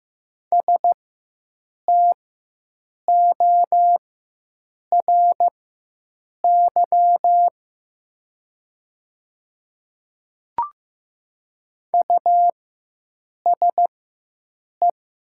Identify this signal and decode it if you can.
Morse code, heard as a keyed tone, about 15 wpm, spaced as for 7 wpm, 700 Hz. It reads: STORY USE
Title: 4x Character Spacing - Top 300 Words - 15wpm